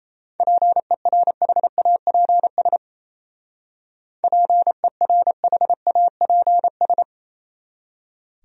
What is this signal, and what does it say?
Morse code, heard as a keyed tone, about 33 wpm, 700 Hz. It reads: PERHAPS PERHAPS
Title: Speed-Racing Top 500 Words 28wpm